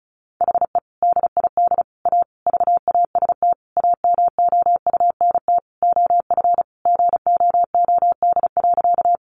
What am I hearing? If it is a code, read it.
Morse code, heard as a keyed tone, about 34 wpm, 700 Hz. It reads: HE DID A VAST AMOUNT OF GOOD.
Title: Sentences from Top 1300 Words 35wpm